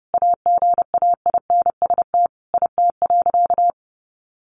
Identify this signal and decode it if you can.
Morse code, heard as a keyed tone, about 30 wpm, 700 Hz. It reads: AGAINST IT.
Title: Sentences from Top 1500 Words 30wpm